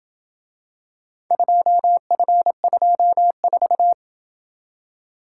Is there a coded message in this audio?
Yes: Morse code from 1.3 to 3.9 s, heard as a keyed tone, about 27 wpm, 700 Hz. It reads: 2F24